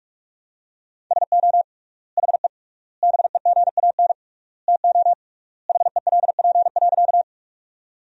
Morse code, heard as a keyed tone, about 41 wpm, 700 Hz. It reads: SO HE BEGAN TO HELP.
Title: 3x Word-Spacing - Sentences from Top 300 Words - 45wpm